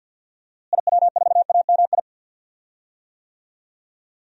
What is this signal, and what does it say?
Morse code, heard as a keyed tone, about 52 wpm, 700 Hz. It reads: IW4AMI
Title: DXCC-100 Call Signs - 6 Characters - Repeated 3x - 50wpm